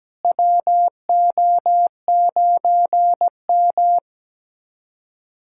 Morse code, heard as a keyed tone, about 17 wpm, 700 Hz. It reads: WO9M
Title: USA Callsign DE Callsign - 4 Characters - 17wpm